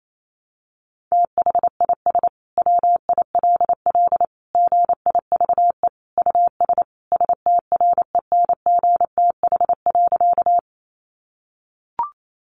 Morse code, heard as a keyed tone, about 28 wpm, 700 Hz. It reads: THIS WILL GIVE US STRENGTH.
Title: Sentences from Top 1300 Words 28wpm